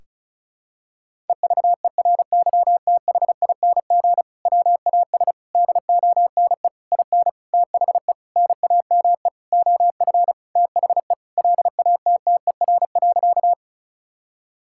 Morse code, heard as a keyed tone, about 36 wpm, 700 Hz. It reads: EVERYTHING WAS DONE IN THE NAME OF THE LATTER.
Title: Sentences from Top 1600 Words 35wpm